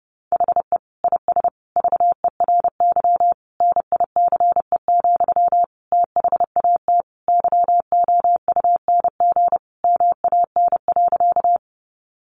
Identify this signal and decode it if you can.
Morse code, heard as a keyed tone, about 30 wpm, 700 Hz. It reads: HE IS VERY NICE, THAT YOUNG MAN.
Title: Sentences from Top 1100 Words 30wpm